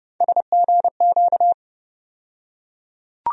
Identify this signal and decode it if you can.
Morse code, heard as a keyed tone, about 30 wpm, 700 Hz. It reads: SGQ
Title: USA Call Sign Postfixes - 3 Letters 30wpm